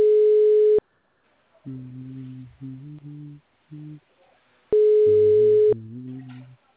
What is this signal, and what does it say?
Ringback tone (Europe, 425 Hz).